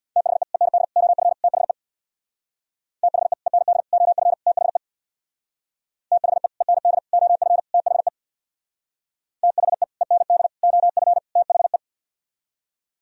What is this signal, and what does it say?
Morse code, heard as a keyed tone, about 60 wpm, 700 Hz. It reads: THE END OF THE THE END OF THE THE END OF THE THE END OF THE
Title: Speed-Racing - Sets of 4 Words 50wpm